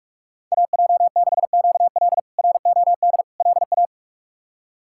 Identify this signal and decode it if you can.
Morse code, heard as a keyed tone, about 42 wpm, 700 Hz. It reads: AJ6QL ROD LA
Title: K1USN SST QSO Practice - 45wpm